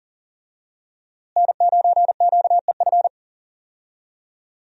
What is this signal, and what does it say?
Morse code, heard as a keyed tone, about 40 wpm, 700 Hz. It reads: N9QEF
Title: USA Call Signs - 5 Characters - Repeated 2x - 40wpm